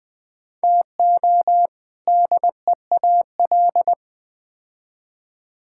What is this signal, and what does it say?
Morse code, heard as a keyed tone, about 20 wpm, 700 Hz. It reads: TO DEAL
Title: Build A Phrase - 4-Word Phrases - 20wpm